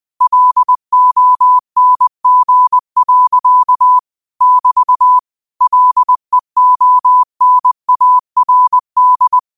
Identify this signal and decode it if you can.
Morse code, heard as a keyed tone, about 20 wpm, 1000 Hz. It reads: LONG. <BT> LEONARD